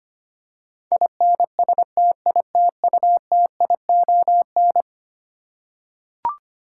Morse code, heard as a keyed tone, about 25 wpm, 700 Hz. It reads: INSTITUTION
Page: morse-code-ninja-practice-25wpm